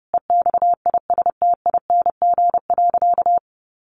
Morse code, heard as a keyed tone, about 30 wpm, 700 Hz. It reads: EXISTING.
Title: Sentences from Top 1500 Words 30wpm